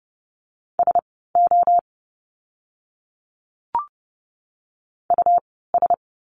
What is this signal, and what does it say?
Morse code, heard as a keyed tone, about 30 wpm, spaced as for 17 wpm, 700 Hz. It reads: SO US
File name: course / 30:10wpm / 230 Lesson 37f - Letter x - Words 30wpm